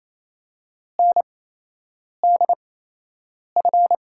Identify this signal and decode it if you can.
Morse code, heard as a keyed tone, about 28 wpm, spaced as for 8 wpm, 700 Hz. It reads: NDF
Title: USA Call Signs - 6 Characters - 8x Character Spacing - 28wpm